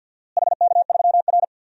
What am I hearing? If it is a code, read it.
Morse code, heard as a keyed tone, about 47 wpm, 700 Hz. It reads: HK3R
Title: DXCC-100 Call Signs - 4 Characters - Repeated 3x - 50wpm